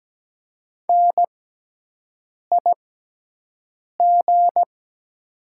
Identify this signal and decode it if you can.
Morse code, heard as a keyed tone, about 17 wpm, spaced as for 6 wpm, 700 Hz. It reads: NIG
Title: 6x Character Spacing - Top 500 Words - 17wpm